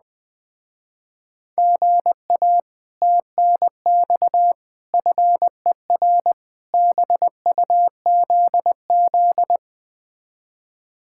Morse code, heard as a keyed tone, about 20 wpm, 700 Hz. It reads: GA TNX FER BUZZ
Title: Speed-Racing Common QSO Phrases Encore - 17wpm